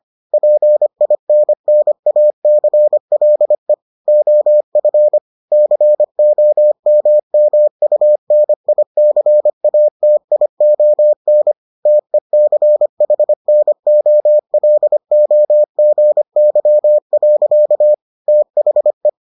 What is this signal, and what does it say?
Morse code, heard as a keyed tone, about 25 wpm, 600 Hz. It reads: PINNACLE OF COMMUNICATION TECHNOLOGY. THE